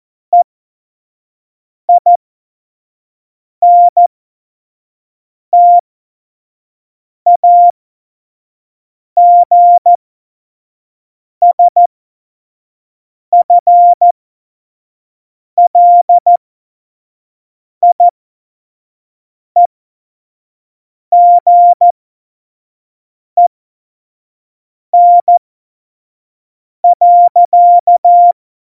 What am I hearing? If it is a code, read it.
Morse code, heard as a keyed tone, about 14 wpm, spaced as for 5 wpm, 700 Hz. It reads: EINTAGSFLIEGEN.
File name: Morsekurs / Witze / Witz 24.Juli 2018, 25 F70